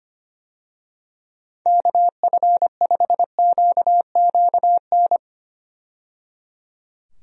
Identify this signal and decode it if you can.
Morse code, heard as a keyed tone, about 25 wpm, 700 Hz. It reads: KF5QQN
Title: USA Call Signs - 6 Characters - 25wpm